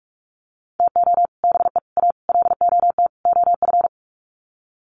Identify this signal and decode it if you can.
Morse code, heard as a keyed tone, about 44 wpm, 700 Hz. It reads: TO BE A LOT OF
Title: Sets of 5 Words Encore 45wpm